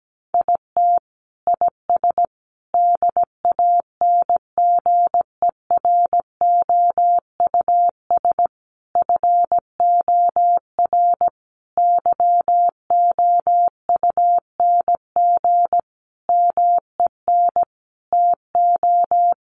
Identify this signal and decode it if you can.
Morse code, heard as a keyed tone, about 17 wpm, 700 Hz. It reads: IT IS DANGEROUS FOR YOUNG MEN TO
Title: Sentences from Top 1200 Words 17wpm